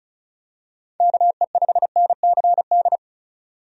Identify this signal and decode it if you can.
Morse code, heard as a keyed tone, about 36 wpm, 700 Hz. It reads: KE5NCD
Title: USA Call Signs - 6 Characters - 35wpm